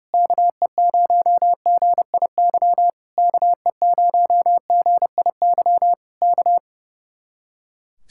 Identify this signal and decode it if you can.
Morse code, heard as a keyed tone, about 30 wpm, 700 Hz. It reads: KE0GIY KE0GIY K
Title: Speed-Racing Common QSO Phrases - 30wpm